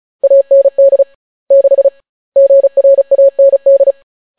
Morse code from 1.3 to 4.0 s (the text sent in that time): 6 GRAND